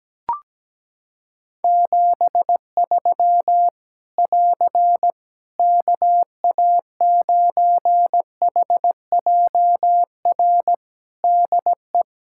Morse code, heard as a keyed tone, about 17 wpm, 700 Hz. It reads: 73 <AR> KA9HJR DE
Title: Practice QSO - Exchange 4 - Ending - 17wpm